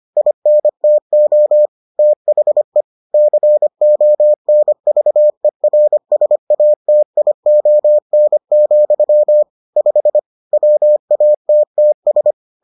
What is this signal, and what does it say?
Morse code, heard as a keyed tone, about 25 wpm, 600 Hz. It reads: INTO THE CONVERSATION, 5 WATTS